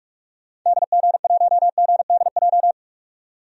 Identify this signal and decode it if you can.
Morse code, heard as a keyed tone, about 43 wpm, 700 Hz. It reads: DG1GDJ
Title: DXCC-100 Call Signs - 6 Characters - 45wpm